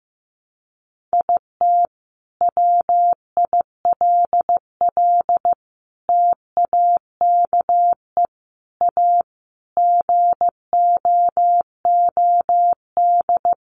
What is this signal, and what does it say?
Morse code, heard as a keyed tone, about 15 wpm, 700 Hz. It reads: IT WILL TAKE A GOOD
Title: Sentences from Top 1300 Words 15wpm